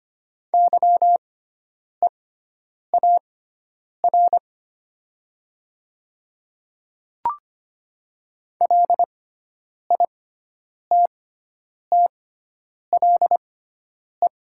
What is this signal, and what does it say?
Morse code, heard as a keyed tone, about 25 wpm, spaced as for 9 wpm, 700 Hz. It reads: YEAR LITTLE
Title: 6x Character Spacing - Top 300 Words - 25wpm